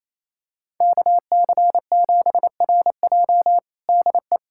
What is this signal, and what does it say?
Morse code, heard as a keyed tone, about 28 wpm, 700 Hz. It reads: KC7RJ DE
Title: USA Callsign DE Callsign - 5 Characters - 28wpm